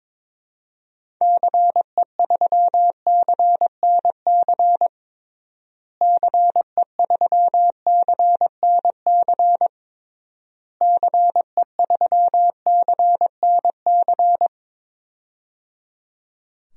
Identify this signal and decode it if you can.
Morse code, heard as a keyed tone, about 22 wpm, 700 Hz. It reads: CE3CNC CE3CNC CE3CNC
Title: DXCC-100 Call Signs - 6 Characters - Repeated 3x - 22wpm